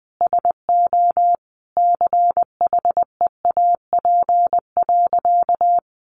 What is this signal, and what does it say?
Morse code, heard as a keyed tone, about 20 wpm, 700 Hz. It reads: SO CHEAP.
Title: Sentences from Top 1200 Words 20wpm